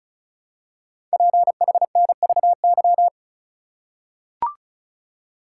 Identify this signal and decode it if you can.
Morse code, heard as a keyed tone, about 36 wpm, 700 Hz. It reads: PHNVY
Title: Code Groups - 5 Letters - 35wpm